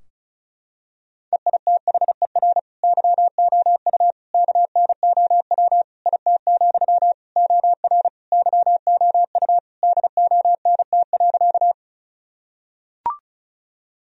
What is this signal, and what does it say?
Morse code, heard as a keyed tone, about 34 wpm, 700 Hz. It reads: EITHER YOU KNOW IT, OR YOU DONT.
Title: Sentences from Top 1000 Words - Review Part 4 - 35wpm